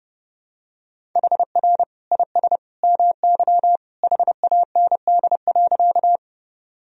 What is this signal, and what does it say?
Morse code, heard as a keyed tone, about 32 wpm, 700 Hz. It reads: HR IS MY HAND.
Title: Sentences from Top 200 Words and Abbreviations 30wpm